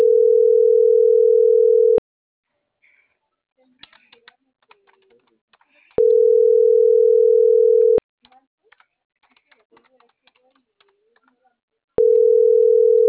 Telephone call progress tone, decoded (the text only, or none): ringback tone